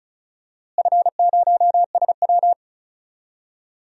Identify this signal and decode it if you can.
Morse code, heard as a keyed tone, about 36 wpm, 700 Hz. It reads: F0SW
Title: DXCC-100 Call Signs - 4 Characters - Repeated 3x - 35wpm